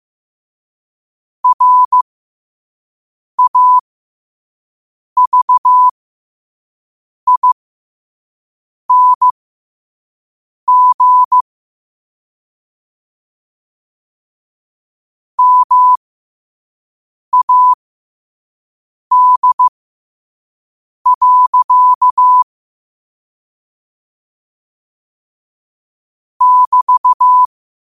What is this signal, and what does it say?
Morse code, heard as a keyed tone, about 15 wpm, spaced as for 5 wpm, 1000 Hz. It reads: RAVING MAD. <BT>
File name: qotdfiles / 5wpm